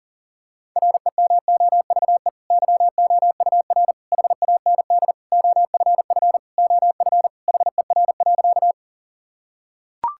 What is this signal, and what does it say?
Morse code, heard as a keyed tone, about 40 wpm, 700 Hz. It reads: REMOVE YOUR HAND OFF OF HER.
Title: Sentences from Top 1000 Words - Review Part 5 - 40wpm